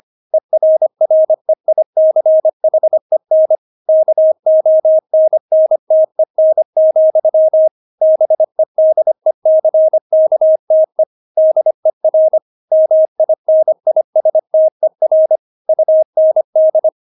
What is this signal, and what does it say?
Morse code, heard as a keyed tone, about 25 wpm, 630 Hz. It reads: ERREICHEN KONNTEN, BEDECKTE DER MINISTER UND
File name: Witz6 25wpm